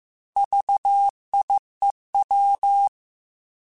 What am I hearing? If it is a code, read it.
Morse code, heard as a keyed tone, about 15 wpm, 800 Hz. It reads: VIEW